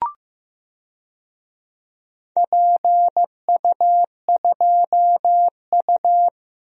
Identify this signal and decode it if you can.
Morse code, heard as a keyed tone, about 15 wpm, 700 Hz. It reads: PU2U
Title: DXCC-100 Call Signs - 4 Characters - Repeated 3x - 15wpm